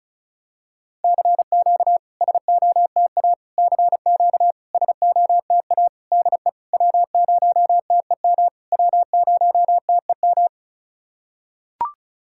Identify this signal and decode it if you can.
Morse code, heard as a keyed tone, about 36 wpm, 690 Hz. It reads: CQ SOTA CQ SOTA DE W0TEM W0TEM
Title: SOTA QSO Practice 35wpm